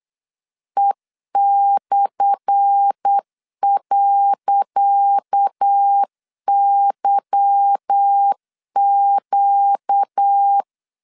Morse code, heard as a keyed tone, about 8 wpm, 800 Hz. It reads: E/.YQ
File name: LEZIONI / 10 SIGNS / 10C DOT 8wpm 3649